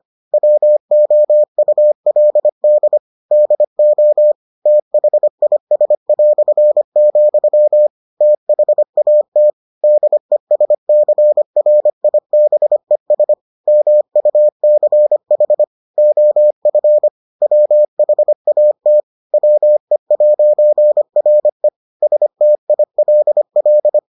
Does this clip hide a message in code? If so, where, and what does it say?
Yes: Morse code from 0.3 to 24.0 s, heard as a keyed tone, about 25 wpm, 600 Hz. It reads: WOULD DO THIS", THAT DESCRIBES MUCH OF WHAT WE'RE STILL